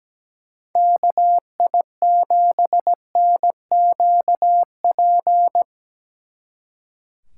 Morse code, heard as a keyed tone, about 19 wpm, 700 Hz. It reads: KI7NQP
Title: Speed Racing - USA 2x3 Call Signs 17wpm